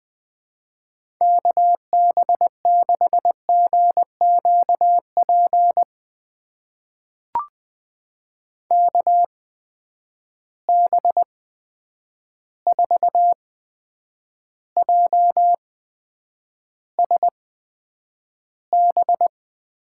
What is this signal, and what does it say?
Morse code, heard as a keyed tone, about 20 wpm, spaced as for 9 wpm, 700 Hz. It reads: KB6GQP KB4JSB
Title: USA Call Signs - 6 Characters - 8x Character Spacing - 20wpm